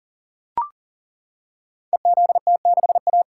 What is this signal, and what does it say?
Morse code, heard as a keyed tone, about 40 wpm, 700 Hz. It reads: EZT6A
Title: Code Groups - 5 Letter:Numbers - 40wpm